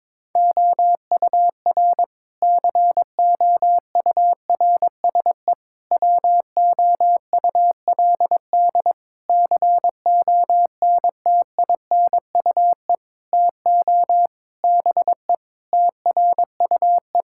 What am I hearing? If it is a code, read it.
Morse code, heard as a keyed tone, about 22 wpm, 700 Hz. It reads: OUR COURSE WOULD CONTINUE TO BE TRUE
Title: Sentences from Top 1000 Words - Review Part 6 - 22wpm